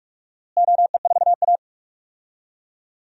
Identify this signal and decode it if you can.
Morse code, heard as a keyed tone, about 46 wpm, 700 Hz. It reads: OE4A
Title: DXCC-100 Call Signs - 4 Characters - Repeated 2x - 45wpm